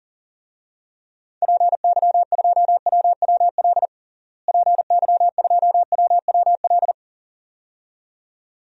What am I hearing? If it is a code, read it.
Morse code, heard as a keyed tone, about 40 wpm, 700 Hz. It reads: PY2WWL PY2WWL
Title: DXCC-100 Call Signs - 6 Characters - Repeated 2x - 40wpm